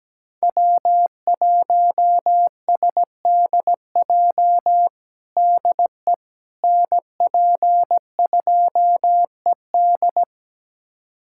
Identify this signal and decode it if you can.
Morse code, heard as a keyed tone, about 17 wpm, 700 Hz. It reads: W1SDJ DE NP2ED
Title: USA Callsign DE Callsign - 5 Characters - 17wpm